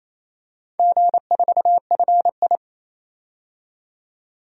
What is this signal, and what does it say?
Morse code, heard as a keyed tone, about 28 wpm, 700 Hz. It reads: G4FI
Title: DXCC-100 Call Signs - 4 Characters - 28wpm